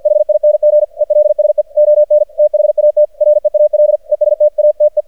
Morse code, heard as a keyed tone, about 48 wpm, 600 Hz. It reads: HING ELSE ON THAT REAL ESTATE
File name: archived / CRSnet